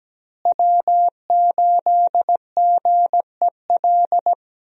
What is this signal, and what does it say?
Morse code, heard as a keyed tone, about 17 wpm, 700 Hz. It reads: W8GEL
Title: USA Call Signs - 5 Characters - Repeated 2x - 17wpm